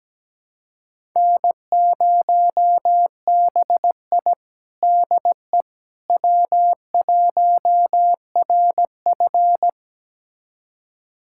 Morse code, heard as a keyed tone, about 17 wpm, 700 Hz. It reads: N0BI DE W1RF
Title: USA Callsign DE Callsign - 4 Characters - 17wpm